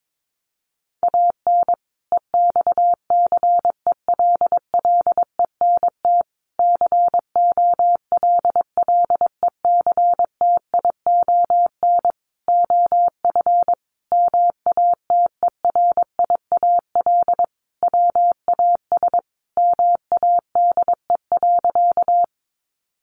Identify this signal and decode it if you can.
Morse code, heard as a keyed tone, about 22 wpm, 700 Hz. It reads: AN EXCELLENT COLLECTION OF MATERIAL WAS MADE.
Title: Sentences from Top 1200 Words 22wpm